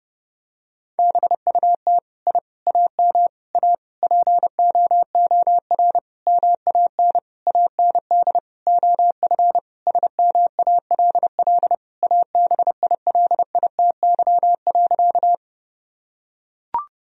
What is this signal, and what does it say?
Morse code, heard as a keyed tone, about 30 wpm, 700 Hz. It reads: BUT I AM A POOR MAN AND OF SMALL ABILITY.
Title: Sentences from Top 1000 Words - Review Part 3 - 30wpm